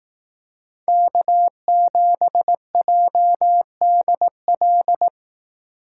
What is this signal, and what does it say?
Morse code, heard as a keyed tone, about 18 wpm, 700 Hz. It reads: K7JDL
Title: Speed Racing - USA 1x3 Call Signs 15wpm